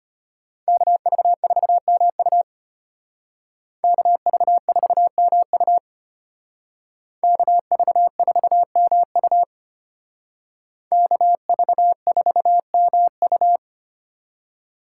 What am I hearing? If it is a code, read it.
Morse code, heard as a keyed tone, about 33 wpm, 700 Hz. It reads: KV4MU KV4MU KV4MU KV4MU